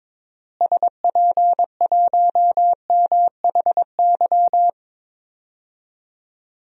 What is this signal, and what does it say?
Morse code, heard as a keyed tone, about 22 wpm, 700 Hz. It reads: SP1MHY
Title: DXCC-100 Call Signs - 6 Characters - Repeated 3x - 22wpm